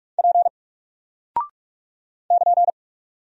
Morse code, heard as a keyed tone, about 45 wpm, 700 Hz. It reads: P <KN>
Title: Single Character (Letter:Number:Punctuation:Prosign) Alt 45wpm